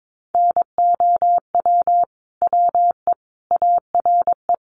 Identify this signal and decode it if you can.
Morse code, heard as a keyed tone, about 22 wpm, 700 Hz. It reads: NOW WE ARE
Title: Sentences from Top 1200 Words 22wpm